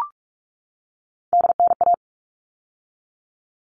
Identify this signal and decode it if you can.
Morse code, heard as a keyed tone, about 44 wpm, 700 Hz. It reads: DNA
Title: All Cut Numbers - 3-Digit 45wpm